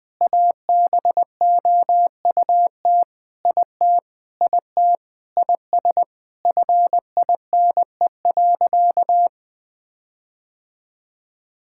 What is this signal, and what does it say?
Morse code, heard as a keyed tone, about 20 wpm, 700 Hz. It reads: ABOUT IT IT IS FINE.
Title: Sentences with 5-Word Phrases - Spoken Beginning - 20wpm